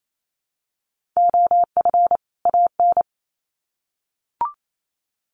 Morse code, heard as a keyed tone, about 28 wpm, 700 Hz. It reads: OF AN
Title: Sets of 2 Words 28wpm